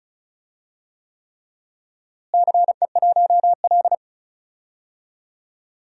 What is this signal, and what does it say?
Morse code, heard as a keyed tone, about 34 wpm, 700 Hz. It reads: CE1L
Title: DXCC-100 Call Signs - 4 Characters - Repeated 3x - 35wpm